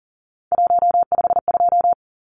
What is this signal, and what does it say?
Morse code, heard as a keyed tone, about 40 wpm, 700 Hz. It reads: 152
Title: Numbers - 3-Digit 40wpm